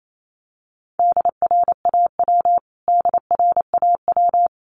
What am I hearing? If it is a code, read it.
Morse code, heard as a keyed tone, about 28 wpm, 700 Hz. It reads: DRAW DRAW
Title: Top 500 Words Repeated 28wpm